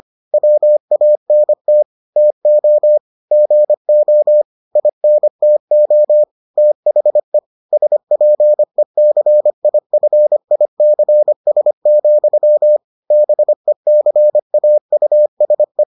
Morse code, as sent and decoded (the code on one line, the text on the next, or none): .-- .- -. - / - --- / --. --- / .. -. - --- / - .... / ... .--. -.-. .. ..-. .. -.-. ... --..-- / -... -.-. .- ..- ... .
WANT TO GO INTO THE SPECIFICS, BECAUSE